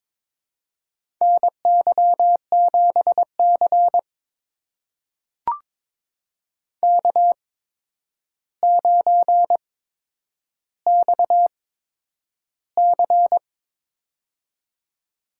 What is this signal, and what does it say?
Morse code, heard as a keyed tone, about 23 wpm, spaced as for 10 wpm, 700 Hz. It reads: NY7C K9XC
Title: USA Call Signs - 4 Characters - 8x Character Spacing - 22wpm